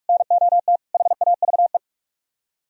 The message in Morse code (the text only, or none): NOT HAVE